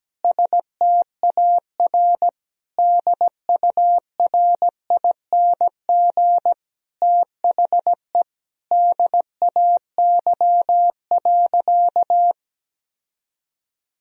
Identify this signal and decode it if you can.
Morse code, heard as a keyed tone, about 17 wpm, 700 Hz. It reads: STAR DURING THE DAY.
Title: Sentences from Top 1000 Words - Review Part 2 - 17wpm